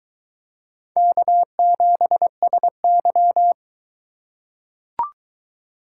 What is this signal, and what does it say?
Morse code, heard as a keyed tone, about 22 wpm, 700 Hz. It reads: K7SY